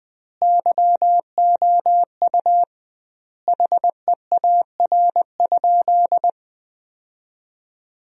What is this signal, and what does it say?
Morse code, heard as a keyed tone, about 21 wpm, 700 Hz. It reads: YOU HEAR?
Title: 2x Word-Spacing - Sentences from Top 300 Words - 20wpm